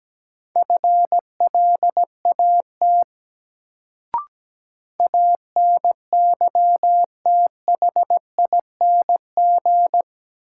Morse code, heard as a keyed tone, about 17 wpm, 700 Hz. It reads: FLAT ANYTHING
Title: Sentences from Top 1000 Words - New Word Drill - 17wpm